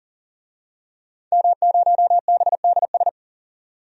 Morse code, heard as a keyed tone, about 38 wpm, 700 Hz. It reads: M0BDS
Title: DXCC-100 Call Signs - 5 Characters - 40wpm